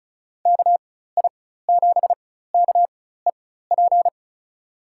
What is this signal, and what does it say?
Morse code, heard as a keyed tone, about 34 wpm, spaced as for 16 wpm, 700 Hz. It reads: KI7KEP
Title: USA Call Signs - 6 Characters - 4x Character Spacing - 35wpm